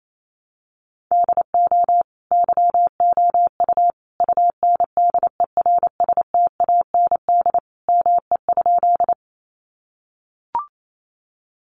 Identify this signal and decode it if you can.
Morse code, heard as a keyed tone, about 28 wpm, 700 Hz. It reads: DO YOU UNDERSTAND ME?
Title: Sentences from Top 600 Words 28wpm